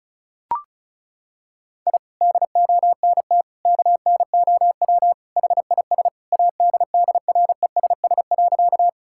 Morse code, heard as a keyed tone, about 36 wpm, 700 Hz. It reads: I DONT KNOW HIS ADDRESS.
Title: Sentences from Top 1000 Words - Review Part 6 - 35wpm